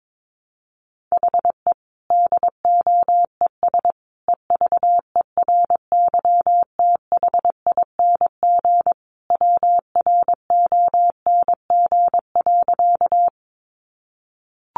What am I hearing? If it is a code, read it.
Morse code, heard as a keyed tone, about 22 wpm, 700 Hz. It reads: HE DOES EVERYTHING WRONG.
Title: Sentences from Top 900 Words 22wpm